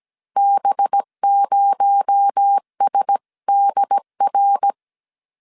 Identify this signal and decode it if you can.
Morse code, heard as a keyed tone, about 17 wpm, 800 Hz. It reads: B0SDR
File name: CW / DET / 1 6 1500C S A 16wpm 2635